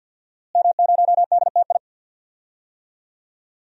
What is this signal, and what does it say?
Morse code, heard as a keyed tone, about 49 wpm, 700 Hz. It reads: M0DTI
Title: DXCC-100 Call Signs - 5 Characters - Repeated 3x - 50wpm